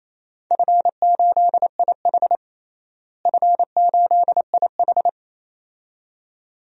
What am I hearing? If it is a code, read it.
Morse code, heard as a keyed tone, about 28 wpm, 700 Hz. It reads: F8IH F8IH
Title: DXCC-100 Call Signs - 4 Characters - Repeated 2x - 28wpm